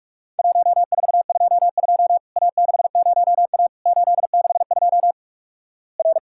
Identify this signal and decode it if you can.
Morse code, heard as a keyed tone, about 48 wpm, 700 Hz. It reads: 1422 A60A 862 R
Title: Worked All Europe DX Contest - QTC Exchange - 45wpm